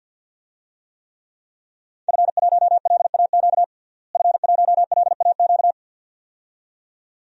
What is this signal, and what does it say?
Morse code, heard as a keyed tone, about 52 wpm, 700 Hz. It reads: F1LAQ F1LAQ
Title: DXCC-100 Call Signs - 5 Characters - Repeated 2x - 50wpm